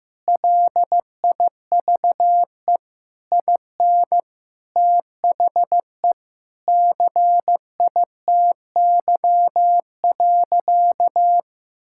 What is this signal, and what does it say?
Morse code, heard as a keyed tone, about 16 wpm, 700 Hz. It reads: LIVE IN THE CITY.
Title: Sentences from Top 1000 Words - Review Part 6 - 15wpm